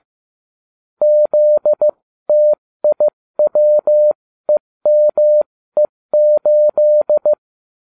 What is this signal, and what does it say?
Morse code, heard as a keyed tone, about 15 wpm, spaced as for 13 wpm, 600 Hz. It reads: ZTIWEME8